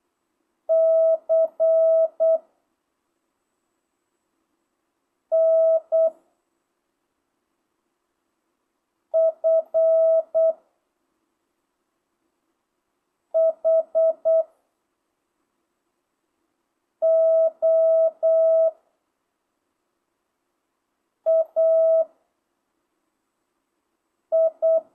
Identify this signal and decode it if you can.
Morse code, heard as a keyed tone, about 8 wpm, 660 Hz. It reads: C N F H O A I
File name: jh8cbh / 2025cw